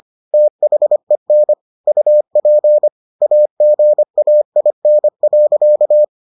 Morse code, heard as a keyed tone, about 25 wpm, 600 Hz. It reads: THEN UP AGAIN.